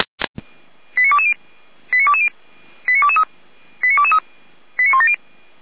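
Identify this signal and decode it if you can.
Five-tone selective call (ZVEI), heard as quick runs of five tones, tones of about 70 ms. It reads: 89220 89220 89223 89223 89170
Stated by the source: ZVEI five-tone selective call received on SDR.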